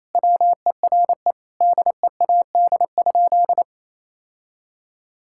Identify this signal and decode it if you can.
Morse code, heard as a keyed tone, about 27 wpm, 700 Hz. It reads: WERE DEAD?
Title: Sentences from Top 1600 Words 28wpm